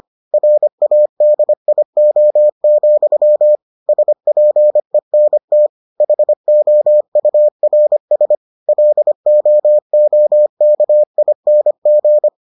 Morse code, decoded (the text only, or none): RADIO, SPENT HOURS LOOKING